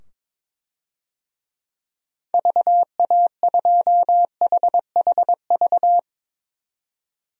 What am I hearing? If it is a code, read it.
Morse code, heard as a keyed tone, about 22 wpm, 700 Hz. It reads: VA2HHV
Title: DXCC-100 Call Signs - 6 Characters - 22wpm